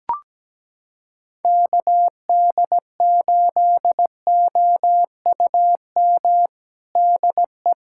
Morse code, heard as a keyed tone, about 17 wpm, 700 Hz. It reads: KD8OUM DE
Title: USA Callsign DE Callsign 17wpm